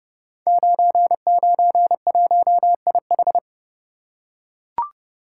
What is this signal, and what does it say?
Morse code, heard as a keyed tone, about 30 wpm, 700 Hz. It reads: 991IH
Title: Code Groups - 5 Letter:Numbers - 30wpm